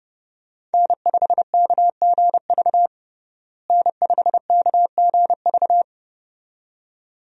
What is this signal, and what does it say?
Morse code, heard as a keyed tone, about 30 wpm, 700 Hz. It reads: N5KGV N5KGV
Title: USA Call Signs - 5 Characters - Repeated 2x - 30wpm